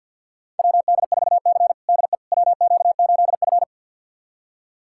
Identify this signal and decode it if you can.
Morse code, heard as a keyed tone, about 50 wpm, 700 Hz. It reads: WD4C DE WQ8F
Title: USA Callsign DE Callsign - 4 Characters - 50wpm